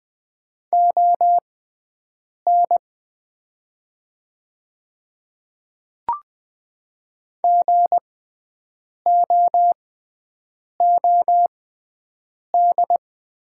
Morse code, heard as a keyed tone, about 20 wpm, spaced as for 7 wpm, 690 Hz. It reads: ON GOOD
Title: 6x Character Spacing - Top 200 Words - 20wpm